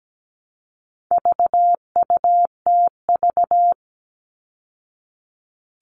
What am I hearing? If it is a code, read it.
Morse code, heard as a keyed tone, about 17 wpm, 700 Hz. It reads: VUTV
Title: All Cut Numbers - 4-Digit 17wpm